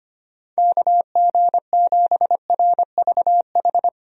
Morse code, heard as a keyed tone, about 25 wpm, 700 Hz. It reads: KG7RVH
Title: USA Call Signs - 6 Characters - 25wpm